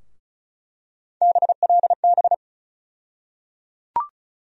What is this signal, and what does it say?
Morse code, heard as a keyed tone, about 34 wpm, 700 Hz. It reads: BLB
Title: USA Call Sign Postfixes - 3 Letters 35wpm